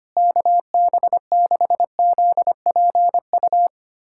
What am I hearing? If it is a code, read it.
Morse code, heard as a keyed tone, about 25 wpm, 700 Hz. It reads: KB6ZPU
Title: USA Callsign DE Callsign 25wpm